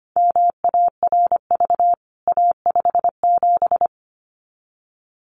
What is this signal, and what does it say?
Morse code, heard as a keyed tone, about 25 wpm, 700 Hz. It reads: MARV A57
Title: CWT Exchange 25wpm